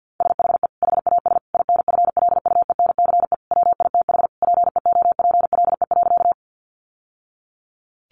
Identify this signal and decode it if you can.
Morse code, heard as a keyed tone, about 50 wpm, 700 Hz. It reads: SHE HAS INFLUENCE WITH PEOPLE.
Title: Sentences from Top 1100 Words 50wpm